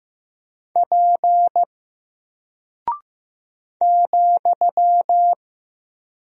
Morse code, heard as a keyed tone, about 15 wpm, 700 Hz. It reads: P ,
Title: Single Character (Letter:Number:Punctuation:Prosign) 15wpm